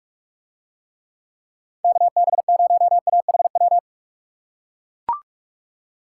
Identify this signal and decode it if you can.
Morse code, heard as a keyed tone, about 42 wpm, 700 Hz. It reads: KB0AHW